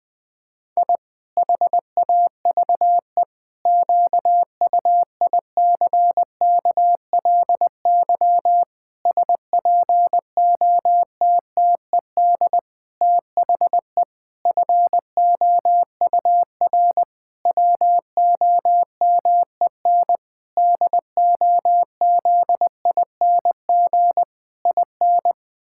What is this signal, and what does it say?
Morse code, heard as a keyed tone, about 20 wpm, 700 Hz. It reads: I HAVE QUICKLY SPOTTED THE FOUR WOMEN DOZING IN
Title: Pangram Challenge v2 - 20wpm